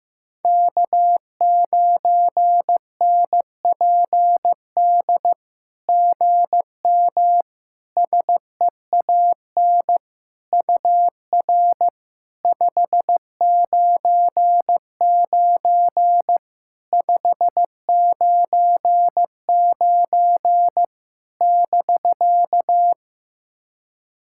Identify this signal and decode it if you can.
Morse code, heard as a keyed tone, about 15 wpm, 700 Hz. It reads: K9NPD GM SEAN UR 599 599 <BK>
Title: SOTA QSO Practice 15wpm